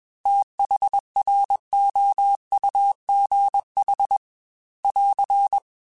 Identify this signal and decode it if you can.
Morse code, heard as a keyed tone, about 21 wpm, 800 Hz. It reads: THROUGH <AR>
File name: TH Runs / TH Runs mp3s / TH Runs @20